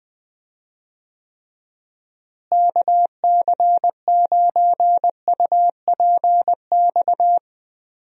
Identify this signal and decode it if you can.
Morse code, heard as a keyed tone, about 20 wpm, 700 Hz. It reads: KC9UPX
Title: USA Call Signs - 6 Characters - Repeated 2x - 20wpm